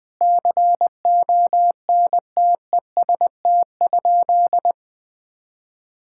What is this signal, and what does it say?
Morse code, heard as a keyed tone, about 20 wpm, 700 Hz. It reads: CONTEST?
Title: Sentences from Top 1600 Words 20wpm